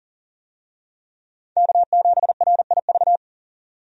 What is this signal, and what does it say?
Morse code, heard as a keyed tone, about 40 wpm, 700 Hz. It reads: K7RIV
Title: USA Call Signs - 5 Characters - 40wpm